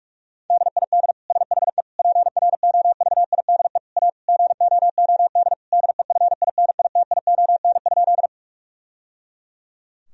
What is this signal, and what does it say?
Morse code, heard as a keyed tone, about 46 wpm, 700 Hz. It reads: DID SHE PROVIDE A GOOD DEFINITION?
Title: Sentences from Top 1000 Words - Review Part 4 - 45wpm